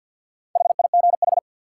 Morse code, heard as a keyed tone, about 53 wpm, 700 Hz. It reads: HIGH
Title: Sentences from Top 200 Words - New Word Drill - 50wpm